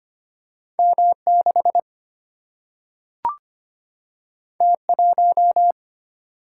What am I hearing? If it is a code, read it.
Morse code, heard as a keyed tone, about 25 wpm, 700 Hz. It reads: M 6 T 1
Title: Single-Letter Single-Number 25wpm